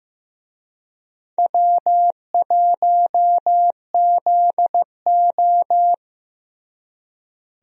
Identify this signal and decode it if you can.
Morse code, heard as a keyed tone, about 15 wpm, 700 Hz. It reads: W1ZO
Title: USA Call Signs - 4 Characters - Repeated 2x - 15wpm